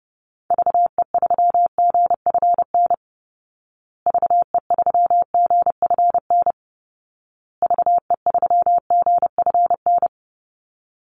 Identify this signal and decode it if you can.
Morse code, heard as a keyed tone, about 30 wpm, 700 Hz. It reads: VE3GFN VE3GFN VE3GFN
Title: Call Signs Repeated 2x - 30wpm